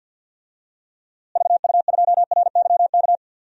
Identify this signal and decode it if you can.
Morse code, heard as a keyed tone, about 50 wpm, 700 Hz. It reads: VU2RYK
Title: DXCC-100 Call Signs - 6 Characters - Repeated 3x - 50wpm